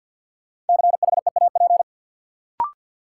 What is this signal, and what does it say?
Morse code, heard as a keyed tone, about 52 wpm, 700 Hz. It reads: CHEAP
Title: Sentences from Top 1200 Words - New Word Drill - 50wpm